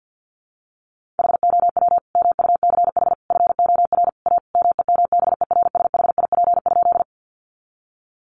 Morse code, heard as a keyed tone, about 50 wpm, 700 Hz. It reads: HOW MUCH FOR A MEMBERSHIP?
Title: Sentences from Top 1400 Words 50wpm